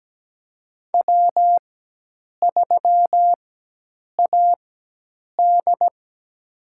Morse code, heard as a keyed tone, about 17 wpm, spaced as for 8 wpm, 700 Hz. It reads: W3AD
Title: USA Call Signs - 4 Characters - 4x Character Spacing - 17wpm